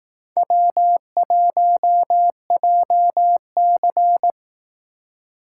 Morse code, heard as a keyed tone, about 18 wpm, 700 Hz. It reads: W1JC
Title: Speed Racing - USA 1x2 Call Signs 15wpm